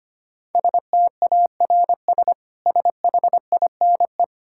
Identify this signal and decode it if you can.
Morse code, heard as a keyed tone, about 25 wpm, 700 Hz. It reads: STARS SHINE
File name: Easy Sentences with Hints - 25wpm